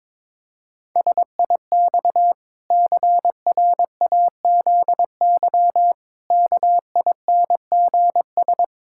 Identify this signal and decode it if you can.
Morse code, heard as a keyed tone, about 22 wpm, 700 Hz. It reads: SIX CRAZY KINGS
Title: Pangram Challenge v2 - 22wpm